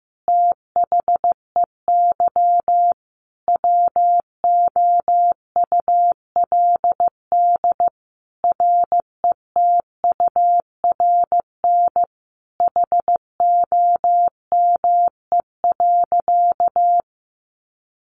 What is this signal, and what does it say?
Morse code, heard as a keyed tone, about 15 wpm, 700 Hz. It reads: THEY WOULD RETURN HOME.
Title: Sentences from Top 1400 Words 15wpm